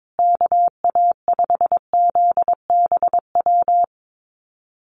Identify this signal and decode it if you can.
Morse code, heard as a keyed tone, about 22 wpm, 700 Hz. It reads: KA5ZBW
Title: USA Call Signs Repeated 2x 22wpm